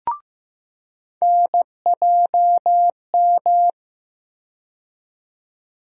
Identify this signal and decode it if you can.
Morse code, heard as a keyed tone, about 15 wpm, 700 Hz. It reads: NJM
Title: USA Call Sign Postfixes - 3 Letters 15wpm